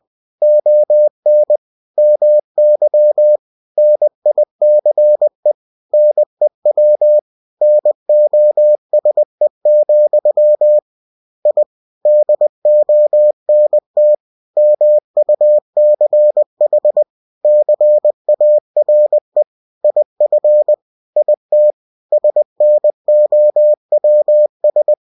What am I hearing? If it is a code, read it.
Morse code, heard as a keyed tone, about 20 wpm, 600 Hz. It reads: ON MY NICE NEW NOSE, I DONT MUCH CARE IF IT SNOWS